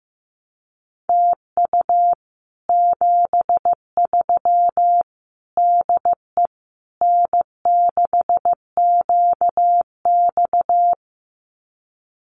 Morse code, heard as a keyed tone, about 15 wpm, 700 Hz. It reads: TU 73 DE N6QX
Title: Common QSO Phrases Encore 15wpm